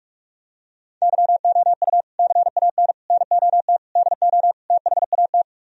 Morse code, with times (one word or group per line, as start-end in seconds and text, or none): 1.0-2.0 YOU
2.2-2.9 CAN
3.1-3.8 NOT
4.0-4.5 DO
4.7-5.4 THAT